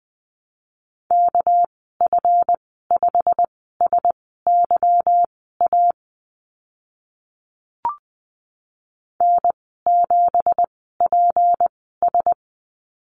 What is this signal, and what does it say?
Morse code, heard as a keyed tone, about 20 wpm, spaced as for 15 wpm, 700 Hz. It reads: KF5SYA N7PS
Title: 200 Lesson 32g - Letter k - Call Signs 20wpm